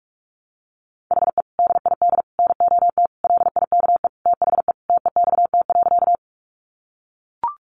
Morse code, heard as a keyed tone, about 44 wpm, 700 Hz. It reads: HE DID NOT LIKE THE TEXT.
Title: Sentences from Top 1200 Words 45wpm